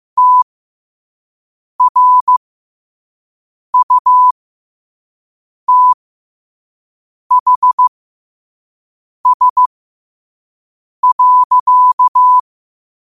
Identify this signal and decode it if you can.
Morse code, heard as a keyed tone, about 15 wpm, spaced as for 5 wpm, 1000 Hz. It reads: TRUTHS.
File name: qotdfiles / 5wpm